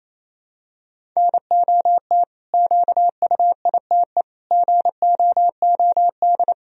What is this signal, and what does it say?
Morse code, heard as a keyed tone, about 28 wpm, 700 Hz. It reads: NOT QUITE GOOD
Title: Sentences from Top 1000 Words - Review Part 1 - 28wpm